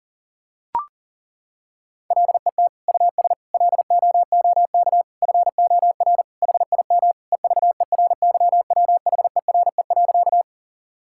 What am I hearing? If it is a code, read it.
Morse code, heard as a keyed tone, about 40 wpm, 700 Hz. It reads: LET US LOOK FOR HIM EVERYWHERE.
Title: Sentences from Top 1600 Words 40wpm